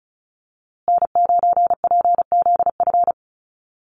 Morse code, heard as a keyed tone, about 34 wpm, 700 Hz. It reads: N9PZF
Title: USA Call Signs Repeated 3x 35wpm